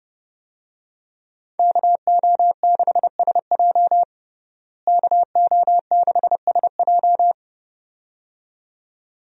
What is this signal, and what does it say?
Morse code, heard as a keyed tone, about 32 wpm, 700 Hz. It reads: KO6SJ KO6SJ
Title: USA Call Signs - 5 Characters - Repeated 2x - 30wpm